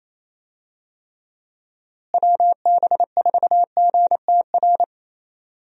Morse code, heard as a keyed tone, about 28 wpm, 700 Hz. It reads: WB4GTR
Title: USA Call Signs - 6 Characters - Repeated 3x - 28wpm